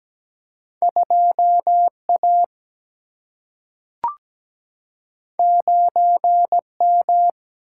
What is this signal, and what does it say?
Morse code, heard as a keyed tone, about 17 wpm, 700 Hz. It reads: 2 A 9 M
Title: Single-Letter Single-Number 17wpm